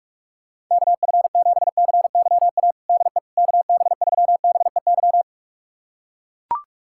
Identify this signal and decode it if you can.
Morse code, heard as a keyed tone, about 42 wpm, 700 Hz. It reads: KF7CYA DE KB3BEY
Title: USA Callsign DE Callsign - 6 Characters - 45wpm